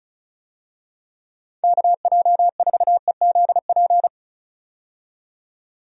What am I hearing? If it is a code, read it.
Morse code, heard as a keyed tone, about 36 wpm, 700 Hz. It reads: KJ4EZP
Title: USA Call Signs - 6 Characters - 35wpm